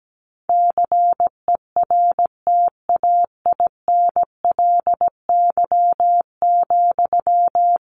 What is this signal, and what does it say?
Morse code, heard as a keyed tone, about 17 wpm, 700 Hz. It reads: CERTAINLY,
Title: Sentences from Top 1500 Words 17wpm